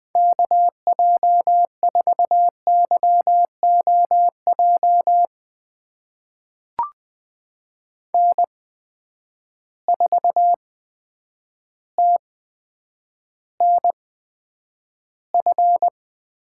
Morse code, heard as a keyed tone, about 20 wpm, spaced as for 9 wpm, 700 Hz. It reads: KJ4YOJ N4TNF
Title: USA Call Signs - 8x Character Spacing - 20wpm